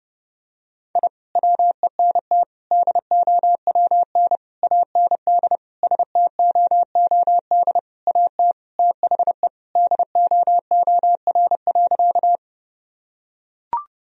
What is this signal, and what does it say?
Morse code, heard as a keyed tone, about 30 wpm, 700 Hz. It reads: I WENT DOWN AND STOOD AT THE DOOR.
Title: Sentences from Top 1000 Words - Review Part 2 - 30wpm